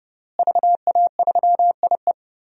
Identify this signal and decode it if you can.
Morse code, heard as a keyed tone, about 31 wpm, 700 Hz. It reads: VA3IE
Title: DXCC-100 Call Signs - 5 Characters - Repeated 2x - 30wpm